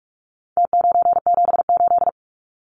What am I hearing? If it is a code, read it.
Morse code, heard as a keyed tone, about 44 wpm, 700 Hz. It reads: T978
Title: Numbers - Contest Style - 4-Digit 45wpm